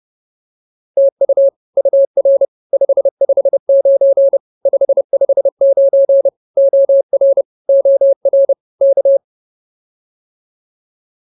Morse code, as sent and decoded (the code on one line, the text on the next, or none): - ..- / ..- .-. / ..... ..... ----. / ..... ..... ----. / --- .-. / --- .-. / -.-
TU UR 559 559 OR OR K